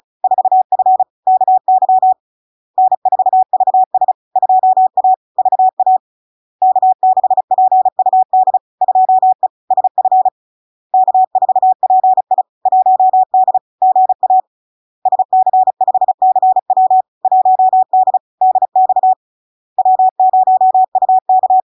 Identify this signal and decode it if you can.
Morse code, heard as a keyed tone, about 36 wpm, 750 Hz. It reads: 4F KY N4VS 2A VA K6PUD 2E SF K4PI 1D GA SC5CW 1D DX W0UK
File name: LARK-FD-2022-Log-in-CW-35wpm-750-Hz